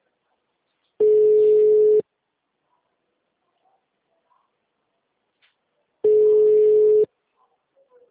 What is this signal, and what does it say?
Ringback tone (Europe, 425 Hz).